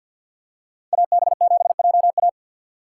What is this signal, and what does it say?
Morse code, heard as a keyed tone, about 47 wpm, 700 Hz. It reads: AB7JA
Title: USA Call Signs - 5 Characters - 4x Character Spacing - 50wpm